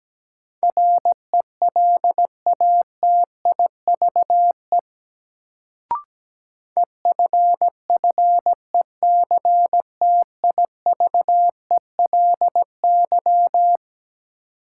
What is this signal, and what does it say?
Morse code, heard as a keyed tone, about 17 wpm, 700 Hz. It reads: RELATIVE EFFECTIVELY
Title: Sentences from Top 1400 Words - New Word Drill - 17wpm